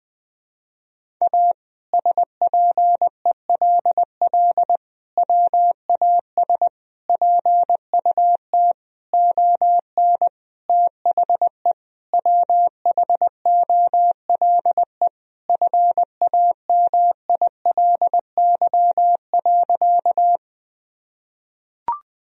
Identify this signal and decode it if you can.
Morse code, heard as a keyed tone, about 21 wpm, 700 Hz. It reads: A SPELL WAS PUT ON THE WHOLE FAMILY.
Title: Sentences from Top 1000 Words - Review Part 1 - 20wpm